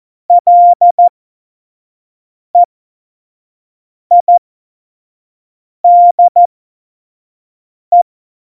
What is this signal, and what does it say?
Morse code, heard as a keyed tone, about 14 wpm, spaced as for 5 wpm, 700 Hz. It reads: LEIDE
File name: Morsekurs / Witze / Witz 13. Februar 2018, 25 F70